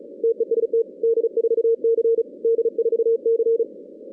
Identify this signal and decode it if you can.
Morse code, heard as a keyed tone, about 35 wpm, 460 Hz. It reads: TEST D4C D4C
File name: 2016 arrl dx cw / 2016-ARRL-DX-CW-K5TR- - 0-41